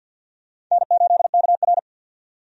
Morse code, heard as a keyed tone, about 52 wpm, 700 Hz. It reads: N8KR
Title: USA Call Signs - 4 Characters - 50wpm